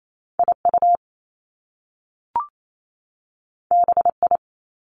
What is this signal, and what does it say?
Morse code, heard as a keyed tone, about 28 wpm, 700 Hz. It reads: I U B I
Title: Top Two Letter Combinations in Words 28wpm